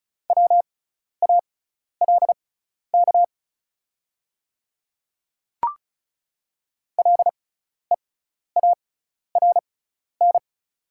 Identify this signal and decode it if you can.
Morse code, heard as a keyed tone, about 36 wpm, spaced as for 12 wpm, 690 Hz. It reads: WALK LEARN
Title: 6x Character Spacing - Top 300 Words - 35wpm